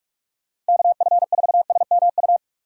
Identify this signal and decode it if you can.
Morse code, heard as a keyed tone, about 59 wpm, spaced as for 52 wpm, 700 Hz. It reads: KF4SMU
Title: Speed Racing - USA 2x3 Call Signs 45wpm